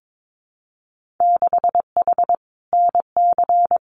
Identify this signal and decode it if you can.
Morse code, heard as a keyed tone, about 22 wpm, 700 Hz. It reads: 6H NC
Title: Winter Field Day Exchanges Encore 22wpm